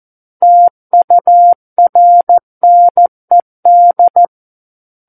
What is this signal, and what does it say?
Morse code, heard as a keyed tone, about 14 wpm, 700 Hz. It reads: TURNED
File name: Book One / 05-Chapter 5 - 14 WPM